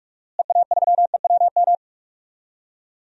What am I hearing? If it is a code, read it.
Morse code, heard as a keyed tone, about 44 wpm, 700 Hz. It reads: EA3EWM
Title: DXCC-100 Call Signs - 6 Characters - Repeated 3x - 45wpm